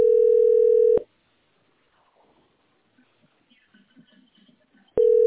Ringback tone (North America), 440 + 480 Hz.